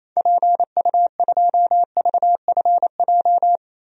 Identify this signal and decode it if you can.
Morse code, heard as a keyed tone, about 28 wpm, 700 Hz. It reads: PU2VFJ
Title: DXCC-100 Call Signs - 6 Characters - Repeated 2x - 28wpm